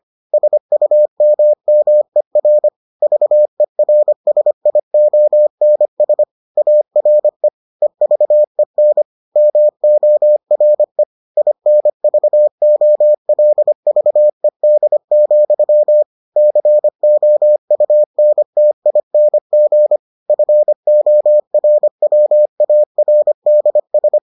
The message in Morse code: SUMMER VERSIONS ARE EVEN MORE INVOLVED, COUNTING FORWARDS

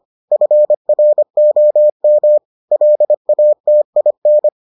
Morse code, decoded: FROM LATIN

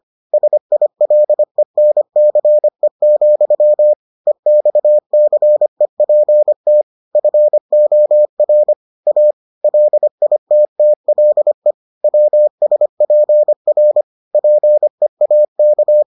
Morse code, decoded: SILENCE, EXCEPT FOR A LITTLE WSPR PEAK

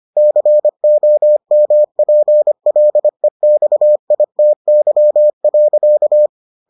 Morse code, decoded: COMPLEXITY.

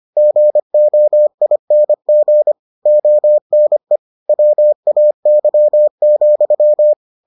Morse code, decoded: GOING ONE WAY,

